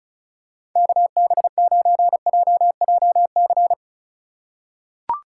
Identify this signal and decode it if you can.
Morse code, heard as a keyed tone, about 36 wpm, 700 Hz. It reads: KB9JJC
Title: USA Call Signs - 6 Characters - 3x Character Spacing - 35wpm